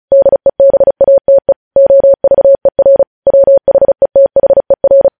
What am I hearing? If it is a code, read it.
Morse code, heard as a keyed tone, about 36 wpm, 550 Hz. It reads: DEBATE OVER WHETHER